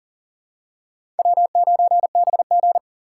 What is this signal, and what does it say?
Morse code, heard as a keyed tone, about 40 wpm, 700 Hz. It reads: W9BG
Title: USA Call Signs - 4 Characters - 6x Character Spacing - 40wpm